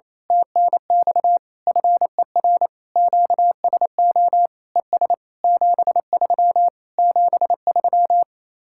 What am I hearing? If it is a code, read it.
Morse code, heard as a keyed tone, about 27 wpm, 700 Hz. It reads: TNX FER QSO ES 73 73
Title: Speed-Racing Common QSO Phrases - 28wpm